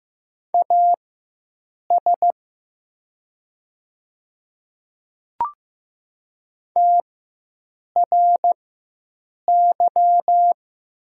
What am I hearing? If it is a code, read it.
Morse code, heard as a keyed tone, about 15 wpm, spaced as for 7 wpm, 700 Hz. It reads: AS TRY
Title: 4x Character Spacing - Top 200 Words - 15wpm